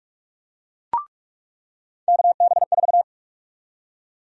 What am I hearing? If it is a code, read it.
Morse code, heard as a keyed tone, about 43 wpm, 700 Hz. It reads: KB4
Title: USA Call Sign Prefixes - 45wpm